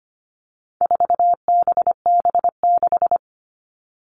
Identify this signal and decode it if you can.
Morse code, heard as a keyed tone, about 25 wpm, 700 Hz. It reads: 4BB6
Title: All Cut Numbers - 4-Digit 25wpm